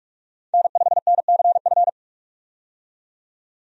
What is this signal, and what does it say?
Morse code, heard as a keyed tone, about 46 wpm, 700 Hz. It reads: N5NCF
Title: USA Call Signs - 5 Characters - 45wpm